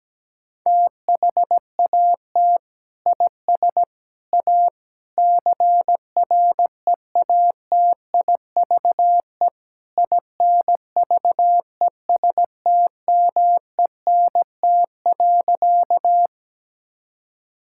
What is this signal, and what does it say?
Morse code, heard as a keyed tone, about 17 wpm, 700 Hz. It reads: THAT IS A CREATIVE INVESTMENT.
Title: Sentences from Top 1000 Words - Review Part 5 - 17wpm